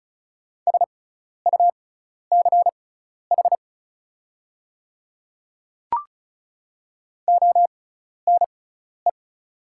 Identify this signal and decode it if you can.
Morse code, heard as a keyed tone, about 36 wpm, spaced as for 12 wpm, 700 Hz. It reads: SUCH ONE
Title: 6x Character Spacing - Top 200 Words - 35wpm